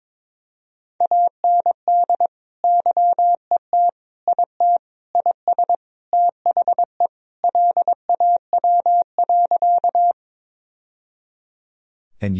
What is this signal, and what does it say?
Morse code, heard as a keyed tone, about 22 wpm, 690 Hz. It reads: AND YET IT IS THE LAW.
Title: Sentences from Top 1000 Words - Review Part 3 - 22wpm